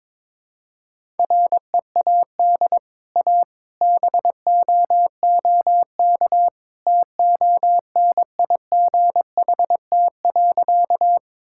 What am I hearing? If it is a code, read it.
Morse code, heard as a keyed tone, about 22 wpm, 700 Hz. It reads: READ A BOOK TONIGHT.
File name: practice / 22wpm / Easy Sentences with Hints - 22wpm